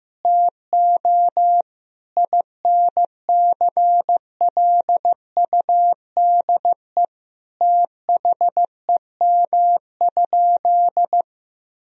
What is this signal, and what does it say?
Morse code, heard as a keyed tone, about 15 wpm, 700 Hz. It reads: TO INCLUDE THEM?
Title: Sentences from Top 1000 Words - Review Part 3 - 15wpm